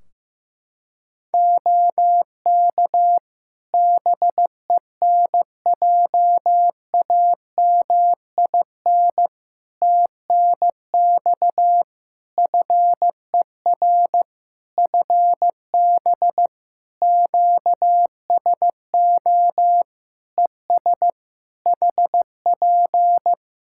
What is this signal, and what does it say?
Morse code, heard as a keyed tone, about 15 wpm, 700 Hz. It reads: OK BENJAMIN TNX FER FB QSO ES HP
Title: Practice QSO - Exchange 4 - Ending - 15wpm